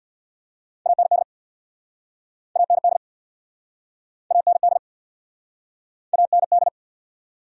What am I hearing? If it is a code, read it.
Morse code, heard as a keyed tone, about 67 wpm, 700 Hz. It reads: AND AND AND AND